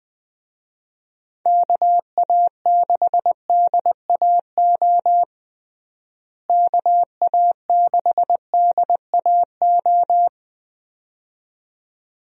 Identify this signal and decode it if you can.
Morse code, heard as a keyed tone, about 20 wpm, 700 Hz. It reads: KA6DAO KA6DAO
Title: USA Call Signs - 6 Characters - Repeated 2x - 20wpm